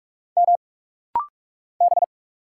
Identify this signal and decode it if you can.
Morse code, heard as a keyed tone, about 44 wpm, 700 Hz. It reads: M B
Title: Single Letter-Number Alt - Rapid-Fire 45wpm